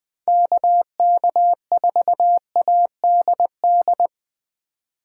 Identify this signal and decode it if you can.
Morse code, heard as a keyed tone, about 21 wpm, 700 Hz. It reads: KK4ADD